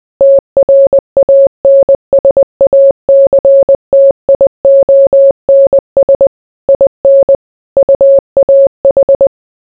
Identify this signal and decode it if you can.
Morse code, heard as a keyed tone, about 20 wpm, 550 Hz. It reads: TRANSACTIONS IN UAH